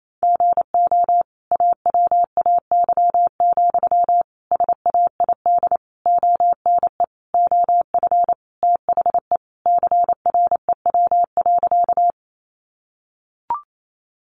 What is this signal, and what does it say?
Morse code, heard as a keyed tone, about 28 wpm, 700 Hz. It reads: GO AWAY, SAID ONE OF THE CREW.
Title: Sentences from Top 1500 Words 28wpm